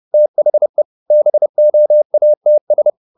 Morse code, heard as a keyed tone, about 30 wpm, 600 Hz. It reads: THE BOATS
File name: the old man and the sea / the old man and the sea 1 30wpm